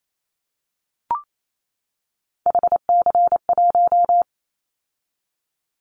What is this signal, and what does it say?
Morse code, heard as a keyed tone, about 28 wpm, 700 Hz. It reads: HC1
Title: Callsign Prefixes for Top 100 DXCC - 28wpm